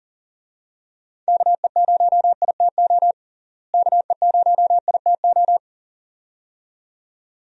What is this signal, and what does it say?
Morse code, heard as a keyed tone, about 37 wpm, 700 Hz. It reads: KE0ITO KE0ITO
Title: USA Call Signs - 6 Characters - Repeated 2x - 40wpm